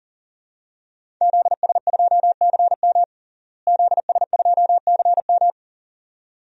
Morse code, heard as a keyed tone, about 40 wpm, 700 Hz. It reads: ZS2CM ZS2CM
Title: DXCC-100 Call Signs - 5 Characters - Repeated 2x - 40wpm